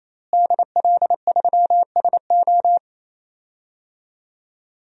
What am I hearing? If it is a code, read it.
Morse code, heard as a keyed tone, about 28 wpm, 700 Hz. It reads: DL3SO
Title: DXCC-100 Call Signs - 5 Characters - Repeated 3x - 28wpm